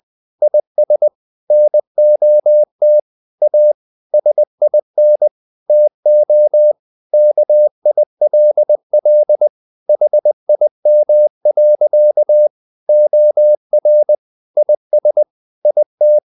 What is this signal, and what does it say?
Morse code, heard as a keyed tone, about 20 wpm, 600 Hz. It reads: IS NOT A SIN TO KILL HIM. OR IS IT
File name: the old man and the sea part 2 20wpm